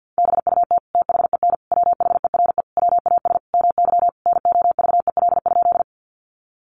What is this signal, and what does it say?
Morse code, heard as a keyed tone, about 50 wpm, 700 Hz. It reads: BUT THEN WHERE WAS MY NOVEL?
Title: Sentences from Top 1300 Words 50wpm